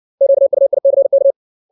Morse code, heard as a keyed tone, about 60 wpm, 550 Hz. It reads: QUICK
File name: 632 words / 632 Words 60WPM